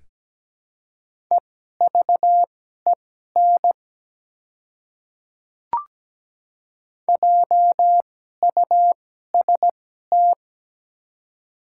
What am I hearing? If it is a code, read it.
Morse code, heard as a keyed tone, about 17 wpm, spaced as for 12 wpm, 700 Hz. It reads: EVEN JUST
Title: 2x Character Spacing - Top 200 Words - 17wpm